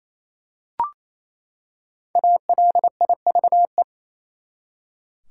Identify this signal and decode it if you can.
Morse code, heard as a keyed tone, about 28 wpm, 700 Hz. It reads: ALIVE
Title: Five Letter Words - 28wpm